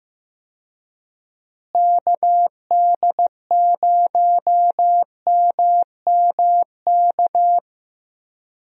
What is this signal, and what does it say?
Morse code, heard as a keyed tone, about 15 wpm, 700 Hz. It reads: KD0MMK
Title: USA Call Signs - 6 Characters - Repeated 3x - 15wpm